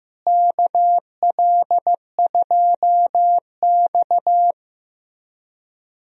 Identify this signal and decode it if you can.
Morse code, heard as a keyed tone, about 15 wpm, 700 Hz. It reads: KL2X
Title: USA Call Signs - 4 Characters - Repeated 3x - 15wpm